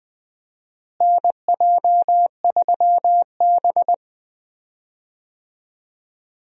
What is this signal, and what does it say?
Morse code, heard as a keyed tone, about 20 wpm, 700 Hz. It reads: NJ3B
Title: USA Call Signs - 4 Characters - Repeated 2x - 20wpm